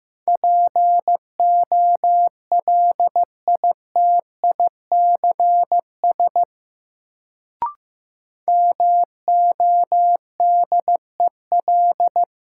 Morse code, heard as a keyed tone, about 15 wpm, 700 Hz. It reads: POLITICS MODEL
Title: Sentences from Top 1000 Words - New Word Drill - 15wpm